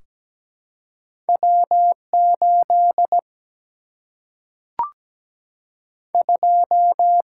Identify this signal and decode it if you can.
Morse code, heard as a keyed tone, about 17 wpm, 700 Hz. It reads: W8 2
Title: Single-Letter Single-Number 17wpm